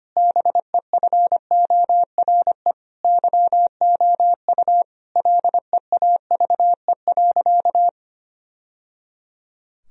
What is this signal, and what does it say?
Morse code, heard as a keyed tone, about 25 wpm, 700 Hz. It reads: BEFORE YOU LEAVE.
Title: Sentences from Top 1000 Words - Review Part 4 - 25wpm